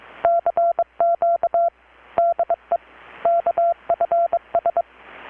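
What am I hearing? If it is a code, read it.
Morse code, heard as a keyed tone, about 23 wpm, 680 Hz. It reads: CQ DE KFS
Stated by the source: Morse from KPH's Night of Nights XX maritime radiotelegraphy event, received in Chiapas, Mexico about 3,800 km away on a Kenwood R-600 with a grounded 10 m long-wire antenna.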